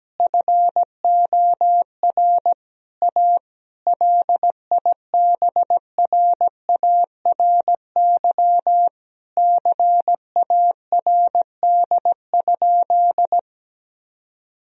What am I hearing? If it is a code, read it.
Morse code, heard as a keyed tone, about 17 wpm, 700 Hz. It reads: FOR A LIBRARY CARD?
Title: Sentences from Top 1000 Words - Review Part 4 - 17wpm